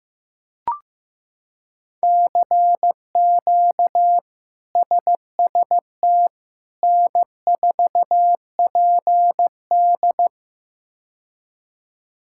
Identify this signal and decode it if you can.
Morse code, heard as a keyed tone, about 16 wpm, 700 Hz. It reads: CQ SST N4PD
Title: K1USN SST QSO Practice - 15wpm